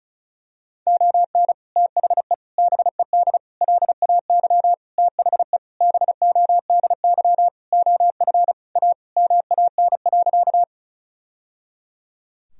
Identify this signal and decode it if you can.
Morse code, heard as a keyed tone, about 36 wpm, 700 Hz. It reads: ON THE BED LAY THE BODY OF A MAN.
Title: Sentences from Top 1000 Words - Review Part 2 - 35wpm